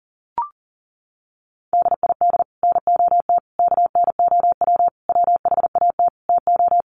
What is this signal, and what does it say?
Morse code, heard as a keyed tone, about 40 wpm, 700 Hz. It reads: DID NOT KNOW WHAT TO
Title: Sets of 5 Words 40wpm